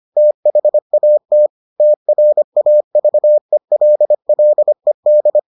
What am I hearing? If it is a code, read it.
Morse code, heard as a keyed tone, about 25 wpm, 600 Hz. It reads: THAT TRAVELLED